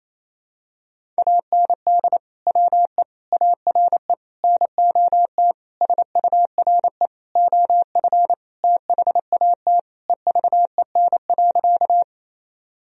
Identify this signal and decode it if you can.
Morse code, heard as a keyed tone, about 28 wpm, 700 Hz. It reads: AND WE ARE NOT SURE OF THAT EVEN.
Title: Sentences from Top 1000 Words - Review Part 1 - 28wpm